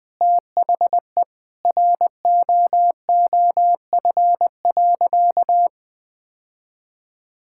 Morse code, heard as a keyed tone, about 21 wpm, 700 Hz. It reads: THE ROOF.
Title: Sentences with 3-Word Phrases - Spoken Beginning - 20wpm